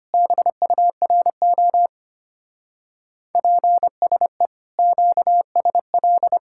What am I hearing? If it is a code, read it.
Morse code, heard as a keyed tone, about 32 wpm, 700 Hz. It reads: BURO PSE QSL